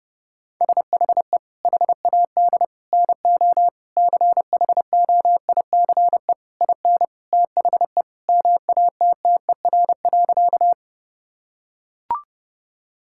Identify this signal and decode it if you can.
Morse code, heard as a keyed tone, about 32 wpm, 700 Hz. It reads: SHE HAD NO CHOICE IN THE MATTER.
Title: Sentences from Top 1000 Words - Review Part 6 - 30wpm